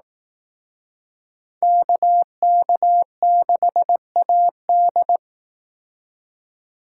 Morse code, heard as a keyed tone, about 18 wpm, 700 Hz. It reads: KK6AD
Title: Speed Racing - USA Call Signs 15wpm